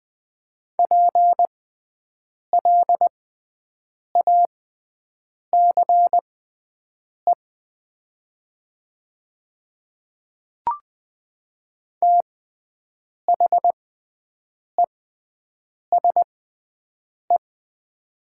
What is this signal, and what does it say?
Morse code, heard as a keyed tone, about 20 wpm, spaced as for 7 wpm, 700 Hz. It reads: PLACE THESE
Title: 6x Character Spacing - Top 300 Words - 20wpm